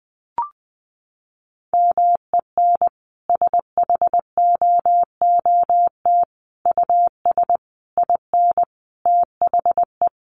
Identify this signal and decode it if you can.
Morse code, heard as a keyed tone, about 20 wpm, 700 Hz. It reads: MEN SHOOT US IN THE
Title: Sentences from Top 1200 Words 20wpm